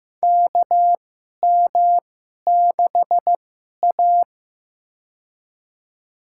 Morse code, heard as a keyed tone, about 15 wpm, spaced as for 11 wpm, 700 Hz. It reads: KM6A
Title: USA Call Signs - 4 Characters - 2x Character Spacing - 15wpm